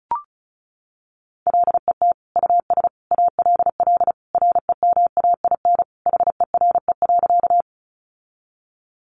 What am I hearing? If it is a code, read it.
Morse code, heard as a keyed tone, about 34 wpm, 700 Hz. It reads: LET US ALL REMAIN HERE.
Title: Sentences from Top 1200 Words 35wpm